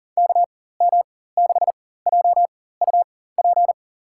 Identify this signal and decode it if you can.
Morse code, heard as a keyed tone, about 40 wpm, spaced as for 19 wpm, 700 Hz. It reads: KM6JUP
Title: USA Call Signs - 6 Characters - 4x Character Spacing - 40wpm